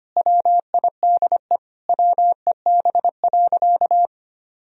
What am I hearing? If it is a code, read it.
Morse code, heard as a keyed tone, about 25 wpm, 700 Hz. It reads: WIDE WEB.
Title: Sentences from Top 1000 Words - Review Part 4 - 25wpm